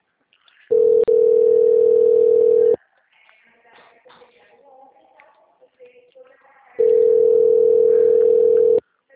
Ringback tone (North America), 440 + 480 Hz.
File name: out- -2001- - - .11